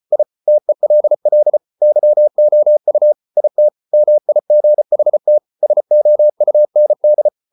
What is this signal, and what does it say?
Morse code, heard as a keyed tone, about 34 wpm, 600 Hz. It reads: I TELL YOU IT MIGHT SOUND